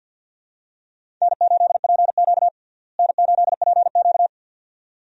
Morse code, heard as a keyed tone, about 52 wpm, 700 Hz. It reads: N8PQ N8PQ
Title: USA Call Signs - 4 Characters - Repeated 2x - 50wpm